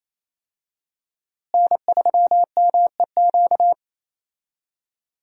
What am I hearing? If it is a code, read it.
Morse code, heard as a keyed tone, about 28 wpm, 700 Hz. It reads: N3MEQ
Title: USA Call Signs - 5 Characters - Repeated 2x - 28wpm